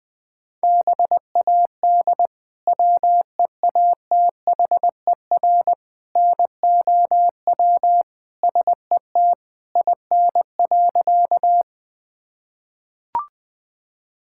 Morse code, heard as a keyed tone, about 20 wpm, 700 Hz. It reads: BAD WEATHER NOW SET IN.
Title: Sentences from Top 1000 Words - Review Part 5 - 20wpm